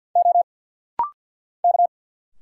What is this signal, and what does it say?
Morse code, heard as a keyed tone, about 52 wpm, 700 Hz. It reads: O K
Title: Single Letters - Rapid-Fire 50wpm